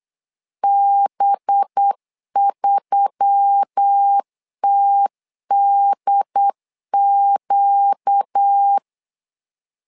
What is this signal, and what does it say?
Morse code, heard as a keyed tone, about 8 wpm, 800 Hz. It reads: B3TDQ
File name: LEZIONI / 5 LBVC / 5 LB 8wpm 2816